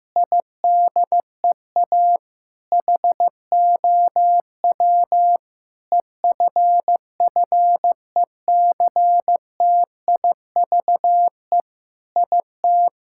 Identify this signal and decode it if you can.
Morse code, heard as a keyed tone, about 16 wpm, 700 Hz. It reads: IDEA HOW EFFECTIVE IT
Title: Sentences from Top 1000 Words - Review Part 6 - 15wpm